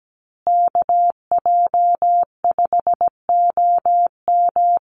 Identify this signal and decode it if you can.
Morse code, heard as a keyed tone, about 17 wpm, 700 Hz. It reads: KJ5OM
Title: USA Call Signs Repeated 2x 17wpm